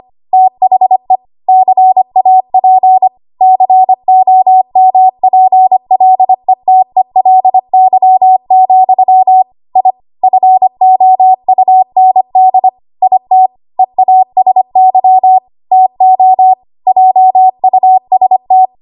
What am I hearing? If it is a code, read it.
Morse code, heard as a keyed tone, about 25 wpm, 750 Hz. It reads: THE CAP COMPLETELY, I FOUND IT EASY TO JUST
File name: Archive / 25 WPM / 25WPM